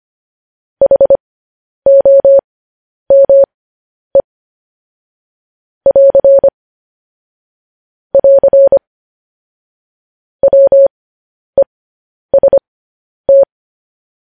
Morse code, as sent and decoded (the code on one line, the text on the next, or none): .... --- -- . / .-.-. / .-.-. / .-- . ... -
HOME <AR> <AR> WEST